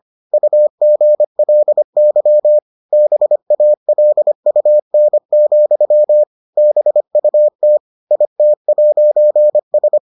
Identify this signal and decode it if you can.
Morse code, heard as a keyed tone, about 25 wpm, 600 Hz. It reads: UGLY BALUN, BUT IT'S